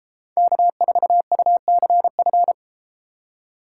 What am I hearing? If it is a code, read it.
Morse code, heard as a keyed tone, about 32 wpm, 700 Hz. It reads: K4UCF